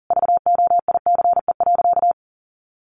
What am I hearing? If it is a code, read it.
Morse code, heard as a keyed tone, about 40 wpm, 700 Hz. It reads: VOICE.
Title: Sentences from Top 500 Words 40wpm